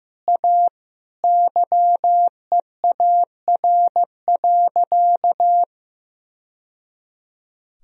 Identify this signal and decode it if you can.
Morse code, heard as a keyed tone, about 15 wpm, 700 Hz. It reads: A YEAR.
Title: Sentences from Top 1000 Words - Review Part 1 - 15wpm